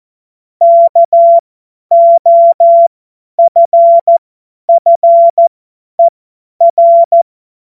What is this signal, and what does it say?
Morse code, heard as a keyed tone, about 14 wpm, spaced as for 10 wpm, 680 Hz. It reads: KOFFER
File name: Morsekurs / Witze / Witz 28. März 2023 50 F70